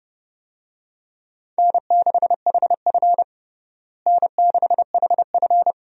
Morse code, heard as a keyed tone, about 32 wpm, 700 Hz. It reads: N6HF N6HF
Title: USA Call Signs - 4 Characters - Repeated 2x - 30wpm